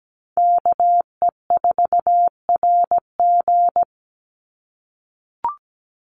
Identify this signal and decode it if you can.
Morse code, heard as a keyed tone, about 17 wpm, 700 Hz. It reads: KE4RG
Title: Call Signs - 17wpm